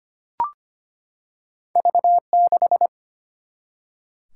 Morse code, heard as a keyed tone, about 25 wpm, 700 Hz. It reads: V6
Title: Single-Letter Single-Number 25wpm